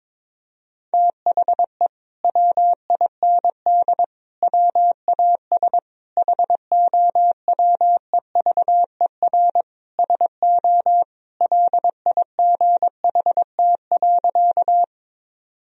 Morse code, heard as a keyed tone, about 22 wpm, 700 Hz. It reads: THE WIND WAS HOWEVER SO LIGHT.
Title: Sentences from Top 1000 Words - Review Part 2 - 22wpm